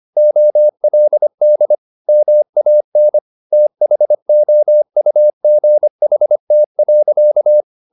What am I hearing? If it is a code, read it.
Morse code, heard as a keyed tone, about 25 wpm, 600 Hz. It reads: OLD MAN THOUGHT.
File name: the old man and the sea part 1 25wpm